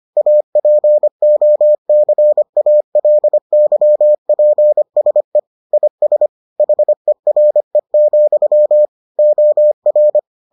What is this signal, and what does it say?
Morse code, heard as a keyed tone, about 25 wpm, 600 Hz. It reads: APOCALYPSE IS HERE, OR